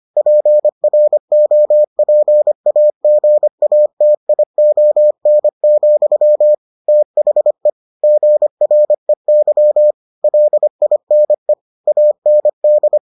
Morse code, heard as a keyed tone, about 25 wpm, 600 Hz. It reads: PROPAGATION, THE GREY LINE AND